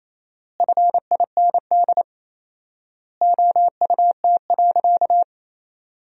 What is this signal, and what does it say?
Morse code, heard as a keyed tone, about 28 wpm, 700 Hz. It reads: FIND OUT.
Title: 4x Word-Spacing - Sentences from Top 200 Words - 28wpm